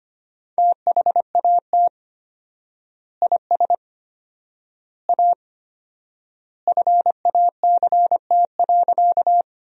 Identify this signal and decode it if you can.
Morse code, heard as a keyed tone, about 25 wpm, 700 Hz. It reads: THAT IS A FACT.
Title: 4x Word-Spacing - Sentences from Top 500 Words - 25wpm